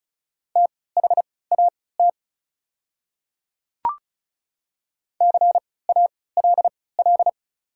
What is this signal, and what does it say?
Morse code, heard as a keyed tone, about 36 wpm, spaced as for 20 wpm, 690 Hz. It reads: THAT CALL
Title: 3x Character Spacing - Top 100 Words - 35wpm